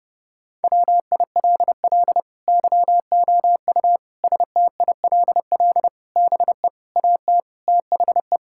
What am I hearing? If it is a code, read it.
Morse code, heard as a keyed tone, about 32 wpm, 700 Hz. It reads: WILL YOU STILL BE AT THE
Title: Sentences from Top 1000 Words - Review Part 5 - 30wpm